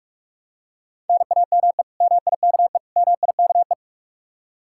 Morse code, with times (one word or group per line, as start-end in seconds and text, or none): 1.1-1.8 NAME
2.0-2.8 MIKE
3.0-3.7 MIKE